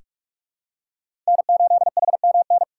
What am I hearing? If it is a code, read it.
Morse code, heard as a keyed tone, about 45 wpm, 700 Hz. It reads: N8HMN
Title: USA Call Signs - 5 Characters - 2x Character Spacing - 45wpm